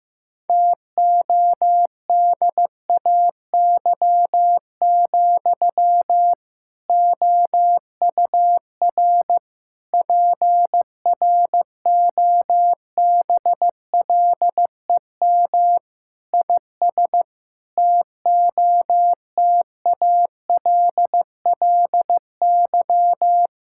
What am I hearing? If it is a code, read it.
Morse code, heard as a keyed tone, about 14 wpm, 700 Hz. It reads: TODAY, OUR PROBLEM IS TOTALLY
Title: Sentences from Top 1600 Words 15wpm